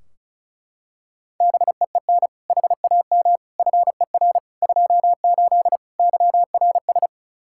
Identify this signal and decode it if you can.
Morse code, heard as a keyed tone, about 34 wpm, 700 Hz. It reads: BEEN HAM FER 28 YRS
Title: Practice QSO - Exchange 3 - More Info - 35wpm